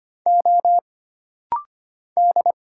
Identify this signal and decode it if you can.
Morse code, heard as a keyed tone, about 25 wpm, 700 Hz. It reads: O D
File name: Single Character (Letter:Number:Punctuation:Prosign) - Rapid-Fire - 25wpm